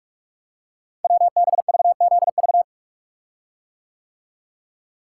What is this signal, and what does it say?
Morse code, heard as a keyed tone, about 44 wpm, 700 Hz. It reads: WBVZV
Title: Code Groups - 5 Letters - 45wpm